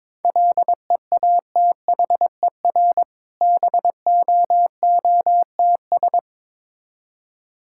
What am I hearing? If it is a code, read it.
Morse code, heard as a keyed tone, about 22 wpm, 690 Hz. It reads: LEATHER BOOTS
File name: Difficult Words with Hints - 22wpm